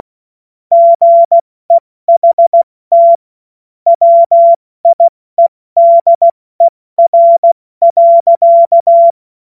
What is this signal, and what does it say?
Morse code, heard as a keyed tone, about 16 wpm, 680 Hz. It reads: GEHT WIEDER.
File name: Morsekurs / Witze / Witz 12 März 2024 70 F80